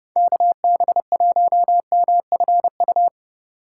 Morse code, heard as a keyed tone, about 32 wpm, 700 Hz. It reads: KB1MFU
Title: USA Call Signs - 6 Characters - 3x Character Spacing - 30wpm